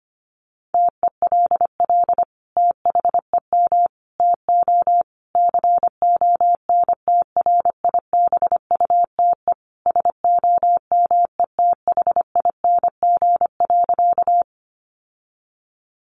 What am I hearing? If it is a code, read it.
Morse code, heard as a keyed tone, about 25 wpm, 700 Hz. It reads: TELL THEM TO CONTRIBUTE SOMETHING.
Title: Sentences from Top 1400 Words 25wpm